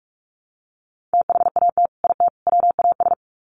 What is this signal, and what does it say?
Morse code, heard as a keyed tone, about 44 wpm, 700 Hz. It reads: THAT IT WAS
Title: Sets of 3 Words 45wpm